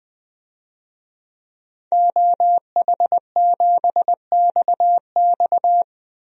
Morse code, heard as a keyed tone, about 20 wpm, 700 Hz. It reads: OH7XX
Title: DXCC-100 Call Signs - 5 Characters - Repeated 2x - 20wpm